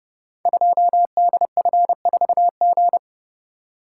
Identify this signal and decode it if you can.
Morse code, heard as a keyed tone, about 29 wpm, 700 Hz. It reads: 2DF4G
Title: Code Groups - 5 Letter:Numbers - 30wpm